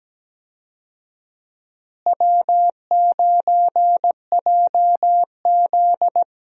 Morse code, heard as a keyed tone, about 17 wpm, 700 Hz. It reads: W9JZ
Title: USA Call Signs - 4 Characters - Repeated 2x - 17wpm